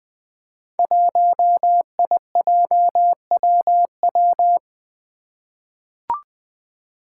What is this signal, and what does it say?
Morse code, heard as a keyed tone, about 20 wpm, 700 Hz. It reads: 1IJWW
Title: Five Challenging Letter-Number Combinations 20wpm